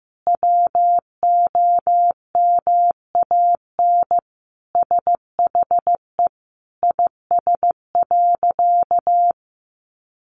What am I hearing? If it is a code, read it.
Morse code, heard as a keyed tone, about 15 wpm, 700 Hz. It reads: WOMAN SHE IS.
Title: Sentences from Top 1400 Words 15wpm